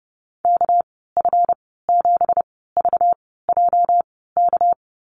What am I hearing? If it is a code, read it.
Morse code, heard as a keyed tone, about 30 wpm, spaced as for 17 wpm, 700 Hz. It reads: KF7VJK
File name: Lesson 37g - Letter x - Call Signs 30wpm